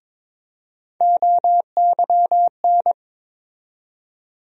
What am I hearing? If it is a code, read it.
Morse code, heard as a keyed tone, about 22 wpm, 700 Hz. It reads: OYN
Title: USA Call Sign Postfixes - 3 Letters 22wpm